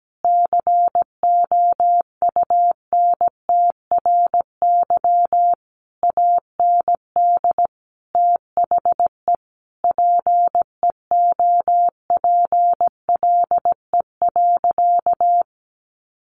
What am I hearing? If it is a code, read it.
Morse code, heard as a keyed tone, about 17 wpm, 700 Hz. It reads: COUNTRY AND THE PEOPLE.
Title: Sentences from Top 1500 Words 17wpm